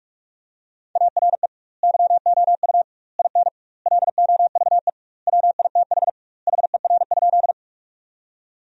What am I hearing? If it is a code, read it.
Morse code, heard as a keyed tone, about 46 wpm, 700 Hz. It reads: ARE YOU IN LOVE WITH HER?
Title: 2x Word-Spacing - Sentences from Top 400 Words - 45wpm